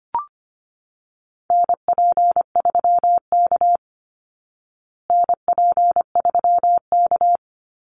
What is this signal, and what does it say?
Morse code, heard as a keyed tone, about 25 wpm, 700 Hz. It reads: NP3K NP3K
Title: Call Signs Repeated 3x - 25wpm